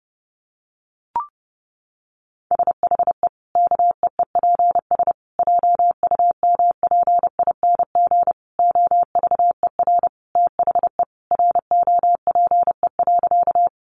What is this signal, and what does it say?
Morse code, heard as a keyed tone, about 30 wpm, 700 Hz. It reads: SHE KEEPS JUMPING OVER THE ROPE.
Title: Easy Phrases 30wpm